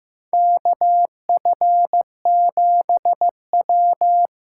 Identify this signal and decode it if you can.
Morse code, heard as a keyed tone, about 16 wpm, 700 Hz. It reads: KF7W
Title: USA Call Signs - 6 Characters - 8x Character Spacing - 15wpm